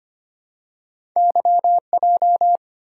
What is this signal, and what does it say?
Morse code, heard as a keyed tone, about 25 wpm, 700 Hz. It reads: YJ
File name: Two Letter-Number Combinations - 25wpm